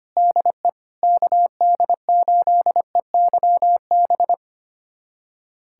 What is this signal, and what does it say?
Morse code, heard as a keyed tone, about 25 wpm, 700 Hz. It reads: DE KD8EYB
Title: USA Callsign DE Callsign - 6 Characters - 25wpm